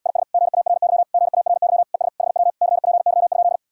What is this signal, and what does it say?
Morse code, heard as a keyed tone, about 78 wpm, 700 Hz. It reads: IS GARY GARY ES HW CPY?